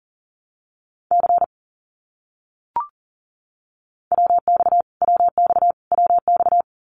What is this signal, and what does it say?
Morse code, heard as a keyed tone, about 40 wpm, 700 Hz. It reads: C WX WX WX
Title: Abbreviations Repeated 3x - 40wpm